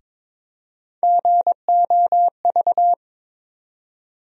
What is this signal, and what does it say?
Morse code, heard as a keyed tone, about 21 wpm, 700 Hz. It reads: GOV
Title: USA Call Sign Postfixes - 3 Letters 22wpm